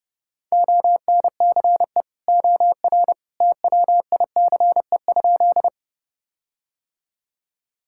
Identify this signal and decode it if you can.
Morse code, heard as a keyed tone, about 30 wpm, 700 Hz. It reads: ONCE OR TWICE?
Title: Sentences from Top 1600 Words 30wpm